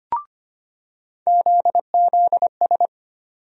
Morse code, heard as a keyed tone, about 25 wpm, 700 Hz. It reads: ZZS
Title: USA Call Sign Postfixes - 3 Letters 25wpm